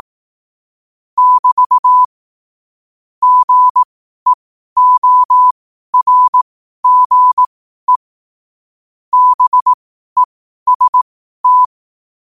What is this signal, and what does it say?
Morse code, heard as a keyed tone, about 18 wpm, spaced as for 12 wpm, 1000 Hz. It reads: <BT> GEORGE BEST
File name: qotdfiles / 12wpm